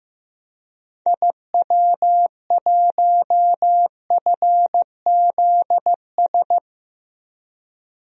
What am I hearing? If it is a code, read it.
Morse code, heard as a keyed tone, about 15 wpm, 700 Hz. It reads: IW1FZS
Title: DXCC-100 Call Signs - 6 Characters - Repeated 2x - 15wpm